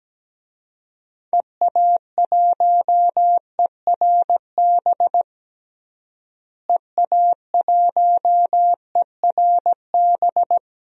Morse code, heard as a keyed tone, about 17 wpm, 700 Hz. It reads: EA1ERB EA1ERB
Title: DXCC-100 Call Signs - 6 Characters - Repeated 3x - 17wpm